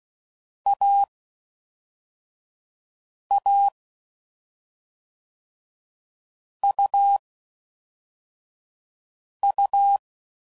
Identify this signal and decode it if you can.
Morse code, heard as a keyed tone, about 16 wpm, spaced as for 4 wpm, 800 Hz. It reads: AA UU
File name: Combinations / Difficult Combos / AUV AUV4